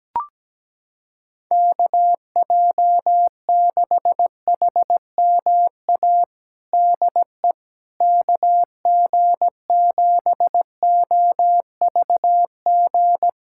Morse code, heard as a keyed tone, about 17 wpm, 700 Hz. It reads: KJ6HMA DE KG7OVG
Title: USA Callsign DE Callsign - 6 Characters - 17wpm